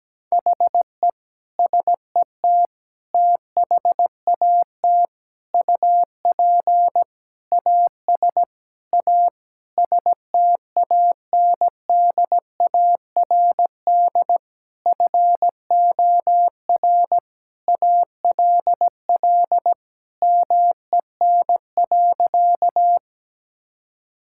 Morse code, heard as a keyed tone, about 17 wpm, 700 Hz. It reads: HE SET THAT UP AS A STANDARD FOR ALL MEN.
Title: Sentences from Top 1000 Words - Review Part 6 - 17wpm